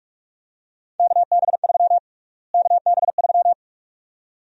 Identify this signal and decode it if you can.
Morse code, heard as a keyed tone, about 42 wpm, 700 Hz. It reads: KB3 KB3
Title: USA Call Sign Prefixes - Repeated 2x - 45wpm